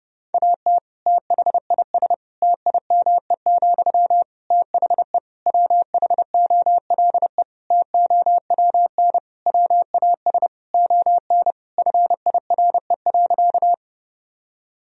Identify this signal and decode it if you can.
Morse code, heard as a keyed tone, about 30 wpm, 700 Hz. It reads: AT THIS TIME, THE WHOLE TOWN WAS ON FIRE.
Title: Sentences from Top 1000 Words - Review Part 2 - 30wpm